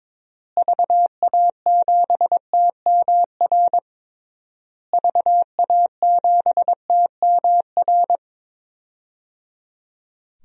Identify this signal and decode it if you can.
Morse code, heard as a keyed tone, about 22 wpm, 700 Hz. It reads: VA7TMR VA7TMR
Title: DXCC-100 Call Signs - 6 Characters - Repeated 3x - 22wpm